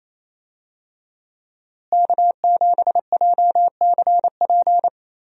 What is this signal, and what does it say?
Morse code, heard as a keyed tone, about 28 wpm, 700 Hz. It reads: K7JCP
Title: USA Call Signs - 5 Characters - Repeated 3x - 28wpm